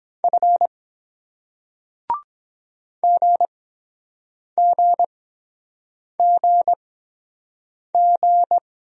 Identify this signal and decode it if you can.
Morse code, heard as a keyed tone, about 23 wpm, spaced as for 7 wpm, 700 Hz. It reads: F GGGG